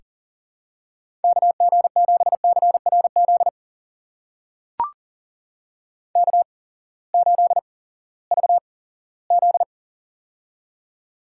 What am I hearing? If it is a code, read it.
Morse code, heard as a keyed tone, about 39 wpm, spaced as for 20 wpm, 700 Hz. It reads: KG7CRZ K8VZ
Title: USA Call Signs - 8x Character Spacing - 40wpm